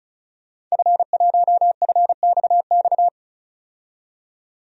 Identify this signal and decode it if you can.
Morse code, heard as a keyed tone, about 34 wpm, 700 Hz. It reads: F1FXX
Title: DXCC-100 Call Signs - 5 Characters - 35wpm